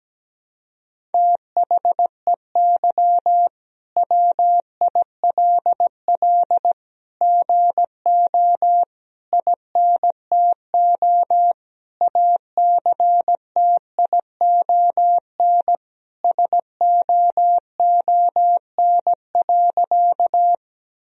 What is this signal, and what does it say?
Morse code, heard as a keyed tone, about 17 wpm, 700 Hz. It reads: THEY WILL GO INTO ACTION SOON.
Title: Sentences from Top 1000 Words - Review Part 4 - 17wpm